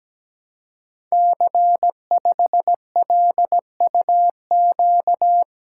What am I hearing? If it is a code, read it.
Morse code, heard as a keyed tone, about 17 wpm, 700 Hz. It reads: C5LUQ
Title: Code Groups - 5 Letter:Numbers - 17wpm